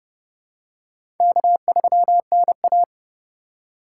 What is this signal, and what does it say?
Morse code, heard as a keyed tone, about 30 wpm, 700 Hz. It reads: K3NA
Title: USA Call Signs - 4 Characters - 3x Character Spacing - 30wpm